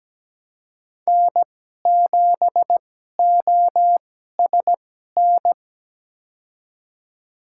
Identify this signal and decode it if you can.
Morse code, heard as a keyed tone, about 17 wpm, spaced as for 12 wpm, 700 Hz. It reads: N7OSN
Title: USA Call Signs - 5 Characters - 2x Character Spacing - 17wpm